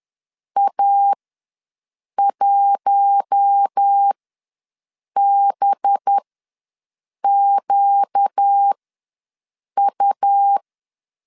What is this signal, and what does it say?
Morse code, heard as a keyed tone, about 10 wpm, spaced as for 6 wpm, 800 Hz. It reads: A1BQU
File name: LEZIONI / 6 PKX0 / 6 P ws 10wpm 3751